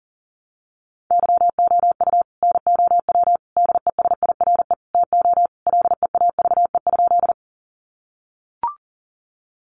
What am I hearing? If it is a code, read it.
Morse code, heard as a keyed tone, about 40 wpm, 700 Hz. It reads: YOU NOW DESIRE TO LEAVE?
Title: Sentences from Top 1100 Words 40wpm